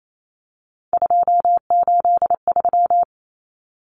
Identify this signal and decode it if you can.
Morse code, heard as a keyed tone, about 28 wpm, 700 Hz. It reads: 283
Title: Common Cut Numbers - 3-Digit 28wpm